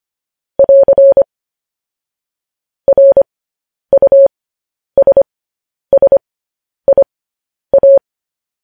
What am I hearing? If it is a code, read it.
Morse code, heard as a keyed tone, about 25 wpm, spaced as for 10 wpm, 550 Hz. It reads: <AR> RUSSIA
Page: news-headlines-in-morse-code-at-10-wpm